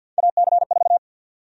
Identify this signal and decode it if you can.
Morse code, heard as a keyed tone, about 52 wpm, 700 Hz. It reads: AC4
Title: USA Call Sign Prefixes - 50wpm